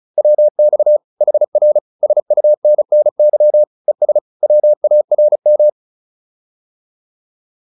Morse code, heard as a keyed tone, about 35 wpm, 600 Hz. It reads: WX HR SUNNY ES WARM
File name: QSO303 35